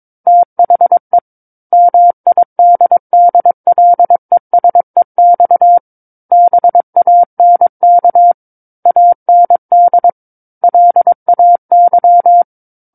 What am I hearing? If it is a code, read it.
Morse code, heard as a keyed tone, about 22 wpm, 700 Hz. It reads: THE MIDDLESEX BANK AND LAY
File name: Book One / 13-Chapter 13 - 22 WPM